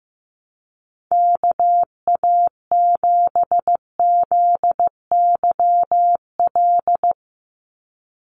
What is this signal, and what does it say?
Morse code, heard as a keyed tone, about 15 wpm, 700 Hz. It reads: KA7ZYL
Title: USA Call Signs 15wpm